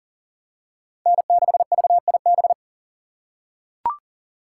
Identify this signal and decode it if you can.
Morse code, heard as a keyed tone, about 40 wpm, 700 Hz. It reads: N6VIB
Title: USA Call Signs - 5 Characters - 2x Character Spacing - 40wpm